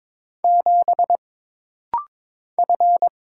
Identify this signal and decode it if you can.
Morse code, heard as a keyed tone, about 22 wpm, 700 Hz. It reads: 7 F
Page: This rapid-fire timing is geared to push your ICR (Instant Character Recognition) skills.